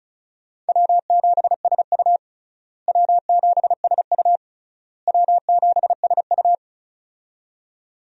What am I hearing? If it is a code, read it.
Morse code, heard as a keyed tone, about 34 wpm, 700 Hz. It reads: W7SU W7SU W7SU
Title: USA Call Signs - 4 Characters - Repeated 3x - 35wpm